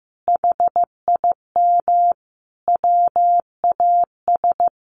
Morse code, heard as a keyed tone, about 14 wpm, 700 Hz. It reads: HIM WAS
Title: Sentences from Top 1400 Words 15wpm